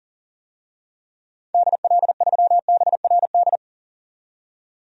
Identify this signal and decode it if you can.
Morse code, heard as a keyed tone, about 37 wpm, 700 Hz. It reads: DL3BRD
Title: DXCC-100 Call Signs - 6 Characters - 40wpm